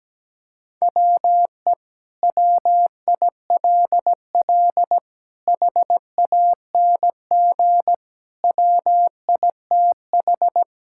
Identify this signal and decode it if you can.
Morse code, heard as a keyed tone, about 17 wpm, 700 Hz. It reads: WE WILL HANG WITH
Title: Sentences from Top 1600 Words 17wpm